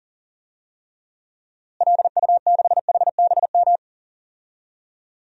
Morse code, heard as a keyed tone, about 38 wpm, 700 Hz. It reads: LU6HBM
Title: DXCC-100 Call Signs - 6 Characters - Repeated 2x - 40wpm